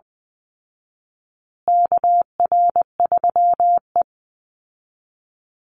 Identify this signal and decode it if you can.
Morse code, heard as a keyed tone, about 19 wpm, 700 Hz. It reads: KR3E
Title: Call Signs Repeated 2x - 20wpm